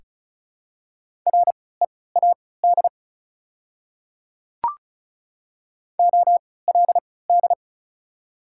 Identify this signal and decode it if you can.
Morse code, heard as a keyed tone, about 36 wpm, spaced as for 20 wpm, 700 Hz. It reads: READ OLD
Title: 3x Character Spacing - Top 200 Words - 35wpm